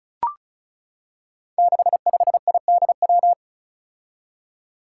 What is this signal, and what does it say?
Morse code, heard as a keyed tone, about 36 wpm, 700 Hz. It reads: 65IDW
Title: Five Challenging Letter-Number Combinations 35wpm